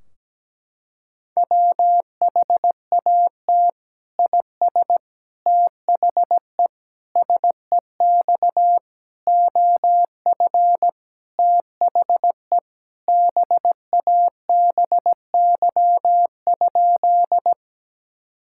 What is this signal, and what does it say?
Morse code, heard as a keyed tone, about 17 wpm, 700 Hz. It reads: WHAT IS THE SEX OF THE BABY?
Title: Sentences from Top 1600 Words 17wpm